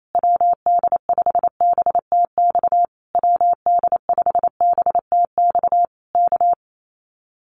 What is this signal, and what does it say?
Morse code, heard as a keyed tone, about 28 wpm, 700 Hz. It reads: WD5BTX WD5BTX K
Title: Common QSO Phrases Encore 28wpm